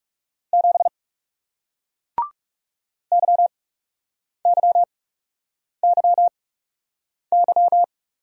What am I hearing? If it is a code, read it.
Morse code, heard as a keyed tone, about 41 wpm, spaced as for 8 wpm, 700 Hz. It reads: 7 YYYY